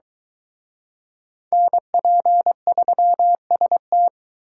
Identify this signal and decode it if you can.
Morse code, heard as a keyed tone, about 23 wpm, 700 Hz. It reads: NP3ST